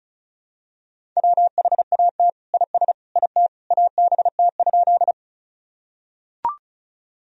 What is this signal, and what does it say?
Morse code, heard as a keyed tone, about 36 wpm, 700 Hz. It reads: WHAT IS IT ABT?
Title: Sentences from Top 100 Words and Abbreviations 35wpm